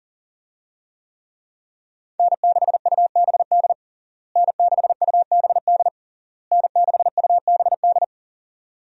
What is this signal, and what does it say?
Morse code, heard as a keyed tone, about 40 wpm, 700 Hz. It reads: N6UBD N6UBD N6UBD
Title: USA Call Signs - 5 Characters - Repeated 3x - 40wpm